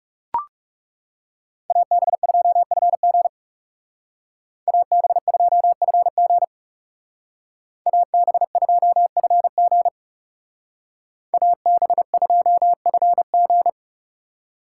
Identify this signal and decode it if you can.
Morse code, heard as a keyed tone, about 39 wpm, 700 Hz. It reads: AB2FG AB2FG AB2FG AB2FG